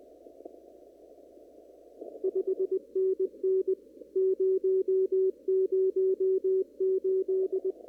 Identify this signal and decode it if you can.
Morse code, heard as a keyed tone, about 20 wpm, 380 Hz. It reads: 5NN 008
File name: CW / kt5j wpx cw-op-k5pi